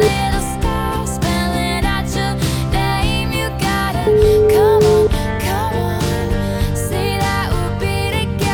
Ringback tone (Europe, 425 Hz), with noise.